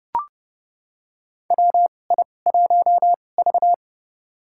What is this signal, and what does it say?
Morse code, heard as a keyed tone, about 30 wpm, spaced as for 22 wpm, 700 Hz. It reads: WI1V
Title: USA Call Signs - 2x Character Spacing - 30wpm